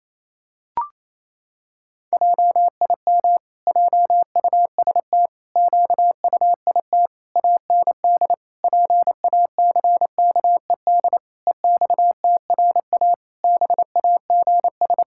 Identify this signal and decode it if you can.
Morse code, heard as a keyed tone, about 28 wpm, 700 Hz. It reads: JIM JUST QUIT AND PACKED EXTRA BAGS
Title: Pangram Challenge v2 - 28wpm